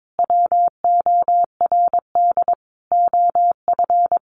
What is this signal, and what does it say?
Morse code, heard as a keyed tone, about 22 wpm, 700 Hz. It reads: WORD OF
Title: Sentences from Top 1500 Words 22wpm